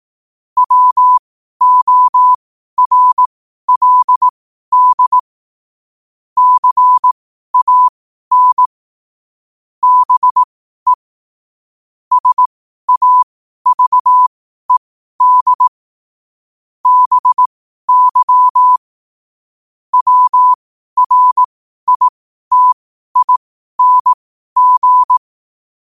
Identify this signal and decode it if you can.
Morse code, heard as a keyed tone, about 18 wpm, spaced as for 12 wpm, 1000 Hz. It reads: WORLD CAN BE SAVED BY WRITING
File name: qotdfiles / 12wpm